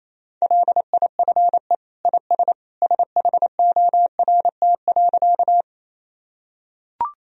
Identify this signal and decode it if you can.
Morse code, heard as a keyed tone, about 27 wpm, 700 Hz. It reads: LIFE IS SHORT.
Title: Sentences from Top 1000 Words - Review Part 2 - 28wpm